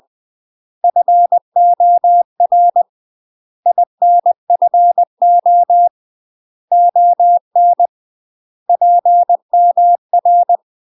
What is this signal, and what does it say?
Morse code, heard as a keyed tone, about 20 wpm, 700 Hz. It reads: FOR INFO ON PMR
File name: UK MASTER QSOs TRAINING / 20WPM / UK MASTER QSOs 0009